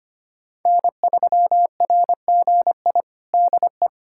Morse code, heard as a keyed tone, about 25 wpm, 700 Hz. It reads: N3RGI DE
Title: USA Callsign DE Callsign - 5 Characters - 25wpm